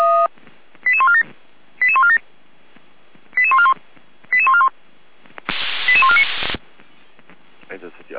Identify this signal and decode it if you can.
Five-tone selective call (ZVEI), heard as quick runs of five tones, tones of about 70 ms, with speech. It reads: 88147 88147 88141 88141 88149